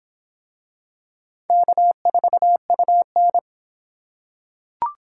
Morse code, heard as a keyed tone, about 25 wpm, 700 Hz. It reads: K4UN